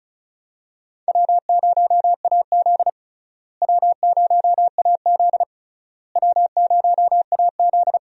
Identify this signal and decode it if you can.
Morse code, heard as a keyed tone, about 34 wpm, 700 Hz. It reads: W0AZ W0AZ W0AZ
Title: USA Call Signs - 4 Characters - Repeated 3x - 35wpm